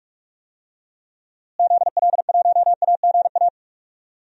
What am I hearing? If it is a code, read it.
Morse code, heard as a keyed tone, about 41 wpm, 700 Hz. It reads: ZL1AGA
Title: DXCC-100 Call Signs - 6 Characters - Repeated 3x - 45wpm